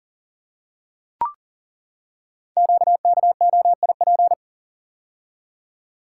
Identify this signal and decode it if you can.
Morse code, heard as a keyed tone, about 42 wpm, 700 Hz. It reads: QKOIP
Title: Code Groups - 5 Letters - 40wpm